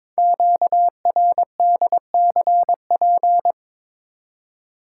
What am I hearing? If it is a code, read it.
Morse code, heard as a keyed tone, about 22 wpm, 700 Hz. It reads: QRDCP
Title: Code Groups - 5 Letter:Numbers - 22wpm